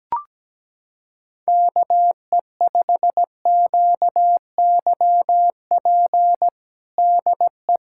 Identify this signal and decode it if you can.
Morse code, heard as a keyed tone, about 17 wpm, 700 Hz. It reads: KE5QYP DE
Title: USA Callsign DE Callsign - 6 Characters - 17wpm